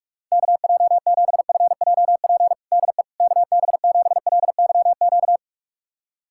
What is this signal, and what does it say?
Morse code, heard as a keyed tone, about 46 wpm, 700 Hz. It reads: KJ7FJP DE KB7LYQ
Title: USA Callsign DE Callsign - 6 Characters - 45wpm